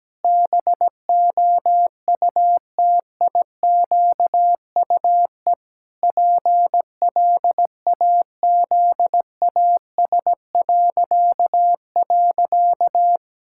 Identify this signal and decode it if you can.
Morse code, heard as a keyed tone, about 17 wpm, 700 Hz. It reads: BOUTIQUE PLAZAS..
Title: Pangram Challenge v2 - 17wpm